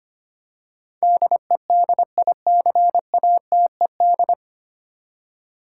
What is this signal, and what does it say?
Morse code, heard as a keyed tone, about 25 wpm, 700 Hz. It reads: DEDICATED
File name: Nine Letter Words - 25wpm